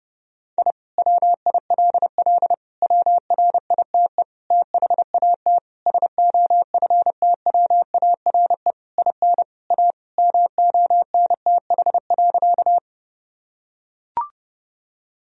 Morse code, heard as a keyed tone, about 30 wpm, 700 Hz. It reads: I WILL WRITE THAT SOFTWARE IN A MONTH.
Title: Sentences from Top 1000 Words - Review Part 3 - 30wpm